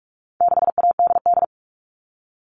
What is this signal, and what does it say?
Morse code, heard as a keyed tone, about 44 wpm, 700 Hz. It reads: 6ADD
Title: All Cut Numbers - 4-Digit 45wpm